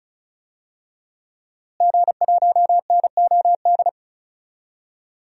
Morse code, heard as a keyed tone, about 34 wpm, 700 Hz. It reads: G1NOD
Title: DXCC-100 Call Signs - 5 Characters - Repeated 2x - 35wpm